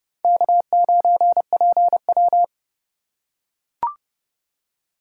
Morse code, heard as a keyed tone, about 28 wpm, 700 Hz. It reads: K9PW